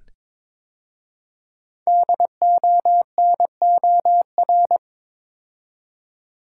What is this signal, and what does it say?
Morse code, heard as a keyed tone, about 22 wpm, 700 Hz. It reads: DONOR
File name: 5 Letter Words - First 3 Letters Given - 22wpm